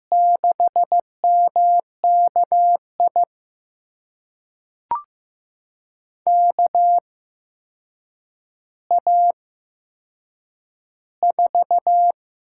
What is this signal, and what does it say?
Morse code, heard as a keyed tone, about 15 wpm, spaced as for 7 wpm, 700 Hz. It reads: 6MKI KA4
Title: USA Call Signs - 6 Characters - 8x Character Spacing - 15wpm